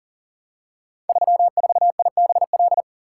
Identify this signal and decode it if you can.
Morse code, heard as a keyed tone, about 39 wpm, 700 Hz. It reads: 34IBL
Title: Code Groups - 5 Letter:Numbers - 40wpm